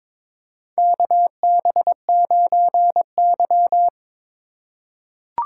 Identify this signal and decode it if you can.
Morse code, heard as a keyed tone, about 22 wpm, 700 Hz. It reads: KB9Y
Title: USA Call Signs - 4 Characters - 2x Character Spacing - 22wpm